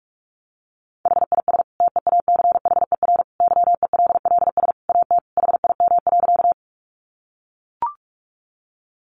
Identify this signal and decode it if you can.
Morse code, heard as a keyed tone, about 44 wpm, 700 Hz. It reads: HIS TEACHER YELLS AT HIM.
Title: Additional Easy Phrases 45wpm